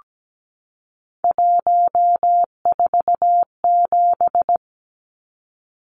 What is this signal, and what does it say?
Morse code, heard as a keyed tone, about 17 wpm, 700 Hz. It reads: 147
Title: Numbers - 3-Digit 17wpm